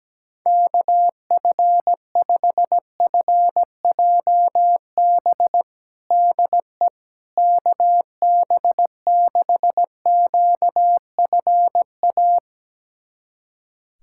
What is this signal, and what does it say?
Morse code, heard as a keyed tone, about 17 wpm, 700 Hz. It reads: KF5FJB DE KB6QFA
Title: USA Callsign DE Callsign - 6 Characters - 17wpm